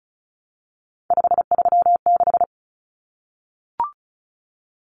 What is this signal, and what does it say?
Morse code, heard as a keyed tone, about 34 wpm, 700 Hz. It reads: 536
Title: Common Cut Numbers - 3-Digit 35wpm